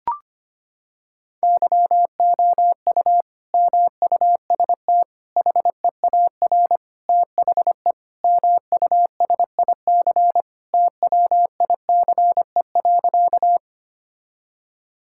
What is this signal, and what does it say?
Morse code, heard as a keyed tone, about 25 wpm, 700 Hz. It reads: YOU MUST HEAR THE MUSIC TWICE.
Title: Sentences from Top 1600 Words 25wpm